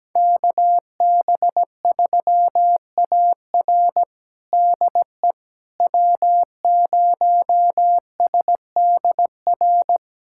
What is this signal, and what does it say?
Morse code, heard as a keyed tone, about 17 wpm, 700 Hz. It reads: KB3AR DE W0SDR
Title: USA Callsign DE Callsign - 5 Characters - 17wpm